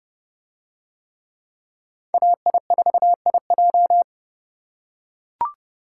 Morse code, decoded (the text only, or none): AI4IJ